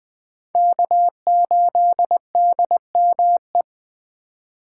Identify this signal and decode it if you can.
Morse code, heard as a keyed tone, about 20 wpm, 700 Hz. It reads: K8DME
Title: Speed Racing - USA Call Signs 17wpm